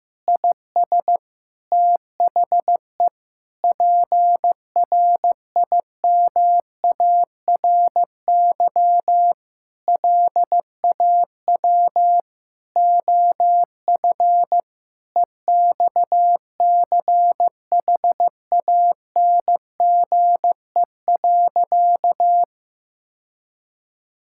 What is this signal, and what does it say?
Morse code, heard as a keyed tone, about 15 wpm, 700 Hz. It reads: IS THE PRIMARY LAW OF EXCHANGE.
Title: Sentences from Top 1000 Words - Review Part 6 - 15wpm